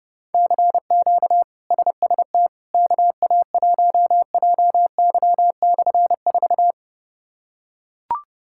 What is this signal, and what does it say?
Morse code, heard as a keyed tone, about 30 wpm, 700 Hz. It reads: CQ SST KA1JY/4
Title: K1USN SST QSO Practice - 30wpm